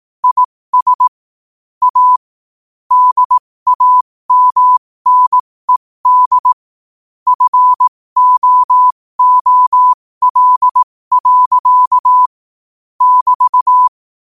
Morse code, heard as a keyed tone, about 18 wpm, spaced as for 15 wpm, 1000 Hz. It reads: IS A DAMNED FOOL. <BT>
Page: New quotes every day in morse code at 15 Words per minute.